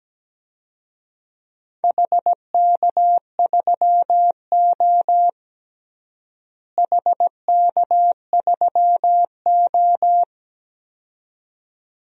Morse code, heard as a keyed tone, about 17 wpm, 700 Hz. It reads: HK3O HK3O
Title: DXCC-100 Call Signs - 4 Characters - Repeated 2x - 17wpm